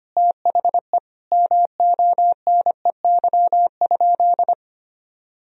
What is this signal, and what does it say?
Morse code, heard as a keyed tone, about 25 wpm, 700 Hz. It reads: THE MONEY?
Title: Sentences from Top 1000 Words - Review Part 2 - 25wpm